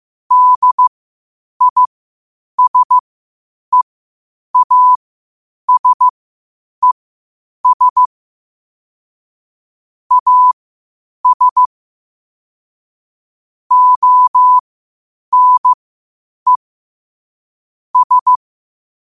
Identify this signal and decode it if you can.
Morse code, heard as a keyed tone, about 15 wpm, spaced as for 8 wpm, 1000 Hz. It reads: DISEASES AS ONE S
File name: qotdfiles / 8wpm